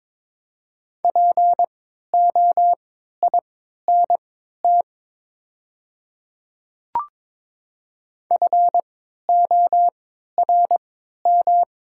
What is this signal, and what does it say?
Morse code, heard as a keyed tone, about 23 wpm, spaced as for 13 wpm, 700 Hz. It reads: POINT FORM
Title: 3x Character Spacing - Top 200 Words - 22wpm